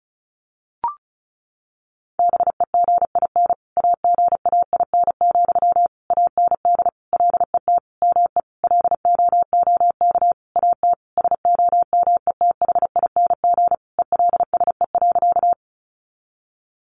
Morse code, heard as a keyed tone, about 34 wpm, 700 Hz. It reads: BEGIN AGAIN, AND LET ME LOOK AT SOMETHING ELSE.
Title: Sentences from Top 800 Words 35wpm